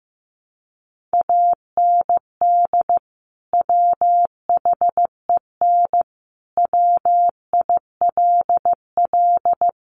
Morse code, heard as a keyed tone, about 15 wpm, 700 Hz. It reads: AND WHEN WILL
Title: Sentences from Top 100 Words 15wpm